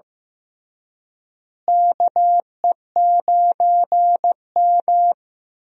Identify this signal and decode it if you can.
Morse code, heard as a keyed tone, about 15 wpm, 700 Hz. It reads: KE9M
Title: USA Call Signs - 4 Characters - Repeated 3x - 15wpm